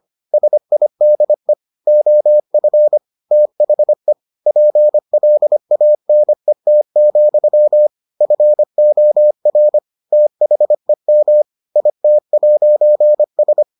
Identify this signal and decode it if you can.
Morse code, heard as a keyed tone, about 25 wpm, 600 Hz. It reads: SIDE OF THE PLANET, FOR THEM IT'S